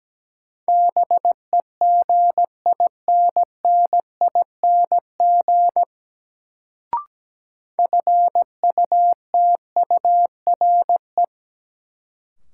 Morse code, heard as a keyed tone, about 17 wpm, 700 Hz. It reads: BEGINNING FUTURE
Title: Sets of 3 Words - New Word Drill - 17wpm